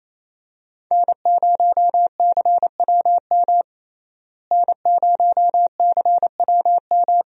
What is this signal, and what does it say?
Morse code, heard as a keyed tone, about 27 wpm, 700 Hz. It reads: N0CWM N0CWM
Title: USA Call Signs - 5 Characters - Repeated 3x - 28wpm